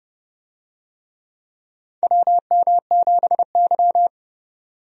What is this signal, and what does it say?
Morse code, heard as a keyed tone, about 29 wpm, 700 Hz. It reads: WM7Y
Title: USA Call Signs - 4 Characters - 30wpm